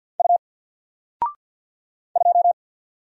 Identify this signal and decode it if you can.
Morse code, heard as a keyed tone, about 50 wpm, 700 Hz. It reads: U 2
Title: Single Letter-Number Alt 50wpm